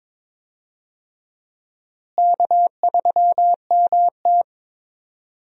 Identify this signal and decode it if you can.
Morse code, heard as a keyed tone, about 22 wpm, 700 Hz. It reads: K3MT
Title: USA Call Signs - 4 Characters - Repeated 2x - 22wpm